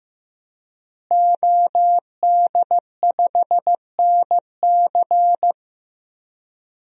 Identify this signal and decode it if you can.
Morse code, heard as a keyed tone, about 15 wpm, 700 Hz. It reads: OD5NC
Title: DXCC-100 Call Signs - 5 Characters - Repeated 2x - 15wpm